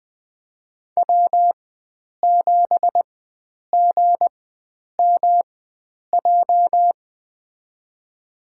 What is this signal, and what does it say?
Morse code, heard as a keyed tone, about 20 wpm, spaced as for 9 wpm, 700 Hz. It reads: W7GMJ
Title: USA Call Signs - 5 Characters - 4x Character Spacing - 20wpm